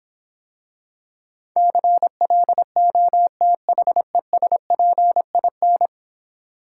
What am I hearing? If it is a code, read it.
Morse code, heard as a keyed tone, about 26 wpm, 700 Hz. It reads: CLOTHESPIN